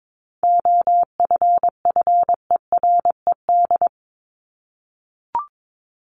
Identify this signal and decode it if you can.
Morse code, heard as a keyed tone, about 22 wpm, 700 Hz. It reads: OFFERED
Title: ED Suffix - Top 100 Words with Base Words 22wpm